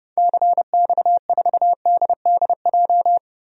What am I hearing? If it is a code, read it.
Morse code, heard as a keyed tone, about 30 wpm, 700 Hz. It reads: CX4DDJ
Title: DXCC-100 Call Signs - 6 Characters - 30wpm